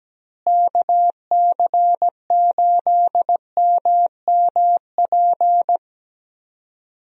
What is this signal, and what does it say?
Morse code, heard as a keyed tone, about 17 wpm, 700 Hz. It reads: KC8MMP
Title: USA Call Signs - 6 Characters - Repeated 3x - 17wpm